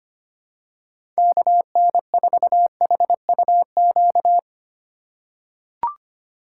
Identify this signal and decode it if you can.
Morse code, heard as a keyed tone, about 25 wpm, 700 Hz. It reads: KN4HUQ
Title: USA Call Signs - 4x Character Spacing - 25wpm